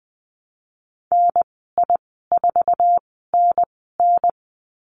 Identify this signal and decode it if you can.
Morse code, heard as a keyed tone, about 20 wpm, spaced as for 14 wpm, 700 Hz. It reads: NI4NN
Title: 048 Lesson 9g - Number 4 - Call Signs 20wpm